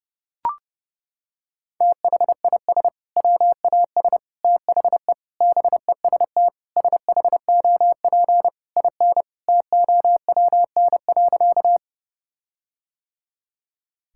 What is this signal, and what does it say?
Morse code, heard as a keyed tone, about 30 wpm, 700 Hz. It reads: THIS WAS THE BEST SHOP IN TOWN.
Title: Sentences from Top 1600 Words 30wpm